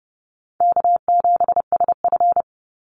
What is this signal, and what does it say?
Morse code, heard as a keyed tone, about 30 wpm, 700 Hz. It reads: K7SF
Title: Easy Call Signs - 30wpm